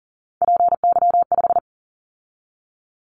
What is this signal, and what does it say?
Morse code, heard as a keyed tone, about 40 wpm, 700 Hz. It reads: PY5
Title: Callsign Prefixes for Top 100 DXCC - 40wpm